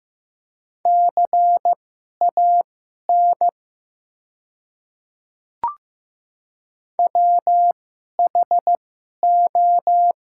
Morse code, heard as a keyed tone, about 15 wpm, spaced as for 11 wpm, 700 Hz. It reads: CAN WHO
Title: 2x Character Spacing - Top 100 Words - 15wpm